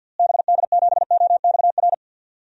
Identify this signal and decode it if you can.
Morse code, heard as a keyed tone, about 52 wpm, 700 Hz. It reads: BD7OXR
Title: DXCC-100 Call Signs - 6 Characters - Repeated 2x - 50wpm